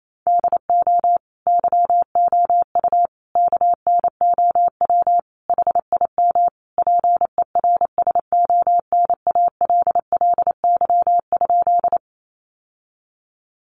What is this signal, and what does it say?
Morse code, heard as a keyed tone, about 28 wpm, 700 Hz. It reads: DO YOU KNOW HIM PERSONALLY?
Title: Sentences from Top 1400 Words 28wpm